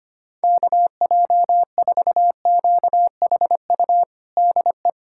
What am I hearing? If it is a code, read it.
Morse code, heard as a keyed tone, about 25 wpm, 700 Hz. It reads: KJ4QHU DE
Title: USA Callsign DE Callsign 25wpm